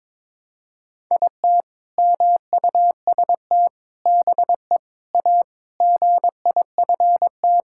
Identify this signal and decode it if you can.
Morse code, heard as a keyed tone, about 22 wpm, 700 Hz. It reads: IT MUST BE A GIFT
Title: Sentences from Top 1000 Words - Review Part 6 - 22wpm